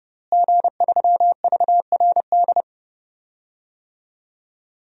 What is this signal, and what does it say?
Morse code, heard as a keyed tone, about 30 wpm, 700 Hz. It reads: G3VRD
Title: DXCC-100 Call Signs - 5 Characters - Repeated 2x - 30wpm